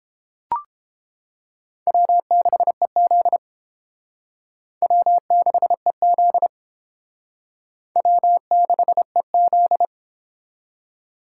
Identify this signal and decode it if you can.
Morse code, heard as a keyed tone, about 30 wpm, 700 Hz. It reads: W6EZ W6EZ W6EZ